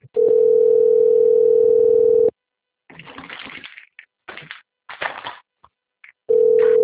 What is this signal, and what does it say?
Ringback tone (North America), 440 + 480 Hz.